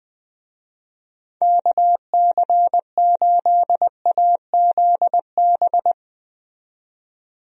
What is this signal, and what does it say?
Morse code, heard as a keyed tone, about 20 wpm, 700 Hz. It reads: KC8AZB
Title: USA Call Signs - 6 Characters - 20wpm